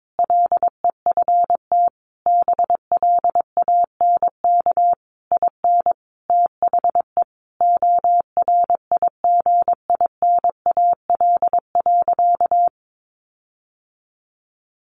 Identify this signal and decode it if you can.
Morse code, heard as a keyed tone, about 22 wpm, 700 Hz. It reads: LEFT BLANK IN THE ORIGINAL.
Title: Sentences from Top 1500 Words 22wpm